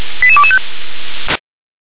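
Five-tone selective call (ZVEI), heard as quick runs of five tones, tones of about 70 ms. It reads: 88225
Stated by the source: ZVEI five-tone call received by SDR.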